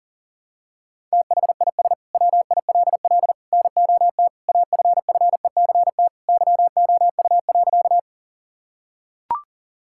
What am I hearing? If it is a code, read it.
Morse code, heard as a keyed tone, about 37 wpm, 700 Hz. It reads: THIS WILL NOT AFFECT YOU.
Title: Sentences from Top 1000 Words - Review Part 6 - 40wpm